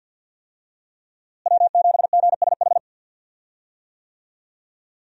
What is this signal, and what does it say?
Morse code, heard as a keyed tone, about 46 wpm, 700 Hz. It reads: W7GSH
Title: USA Call Signs - 5 Characters - Repeated 2x - 50wpm